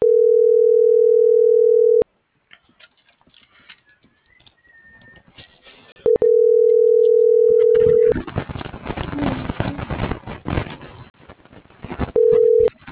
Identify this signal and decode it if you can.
Ringback tone (North America), 440 + 480 Hz.